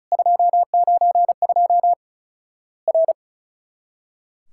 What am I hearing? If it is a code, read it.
Morse code, heard as a keyed tone, about 36 wpm, 700 Hz. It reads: 292 R
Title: Worked All Europe DX Contest - QTC Exchange - 35wpm